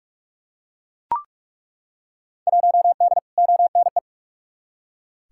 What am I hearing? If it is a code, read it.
Morse code, heard as a keyed tone, about 48 wpm, 700 Hz. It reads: 1D ONE
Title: Field Day Exchanges - 45wpm